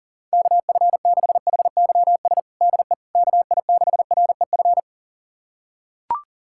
Morse code, heard as a keyed tone, about 40 wpm, 700 Hz. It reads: KF6HYS DE KI6REF
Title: USA Callsign DE Callsign - 6 Characters - 40wpm